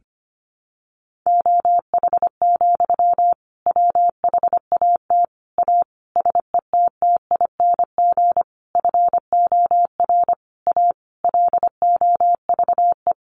Morse code, heard as a keyed tone, about 25 wpm, 700 Hz. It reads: OH, WHAT A SETTING FOR A LOVE
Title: Sentences from Top 1300 Words 25wpm